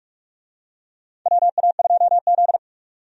Morse code, heard as a keyed tone, about 44 wpm, 700 Hz. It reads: WA2Z
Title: USA Call Signs - 4 Characters - 45wpm